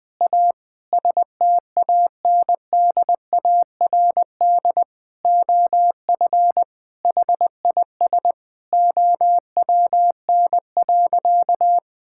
Morse code, heard as a keyed tone, about 20 wpm, 700 Hz. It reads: A STANDARD OF HIS OWN.
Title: Sentences from Top 1000 Words - Review Part 6 - 20wpm